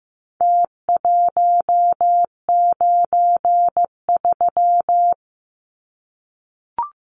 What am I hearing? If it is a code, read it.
Morse code, heard as a keyed tone, about 15 wpm, 700 Hz. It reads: T193
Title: Numbers - Contest Style - 4-Digit 15wpm